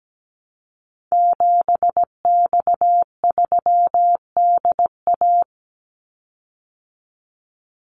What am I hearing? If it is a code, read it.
Morse code, heard as a keyed tone, about 17 wpm, 700 Hz. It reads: 7X3DA
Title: Call Signs Repeated 3x - 17wpm